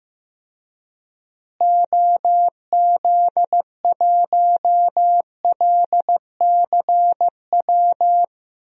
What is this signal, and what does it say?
Morse code, heard as a keyed tone, about 15 wpm, 700 Hz. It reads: OZ1LCW
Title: DXCC-100 Call Signs - 6 Characters - Repeated 2x - 15wpm